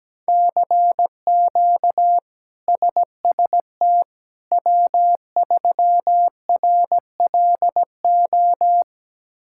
Morse code, heard as a keyed tone, about 17 wpm, 700 Hz. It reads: CQ SST W3RLO
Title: K1USN SST QSO Practice - 17wpm